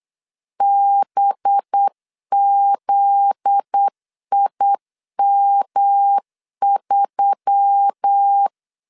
Morse code, heard as a keyed tone, about 8 wpm, 800 Hz. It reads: BZIM3
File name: LEZIONI / 7 GF9Z / 7 GF9Z 8wpm 3213